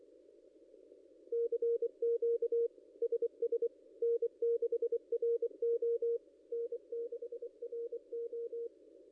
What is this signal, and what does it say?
Morse code, heard as a keyed tone, about 24 wpm, 470 Hz. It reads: CQ SS N6RO N6RO
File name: contests / ss-cw-k5tr-op-n5rz -0800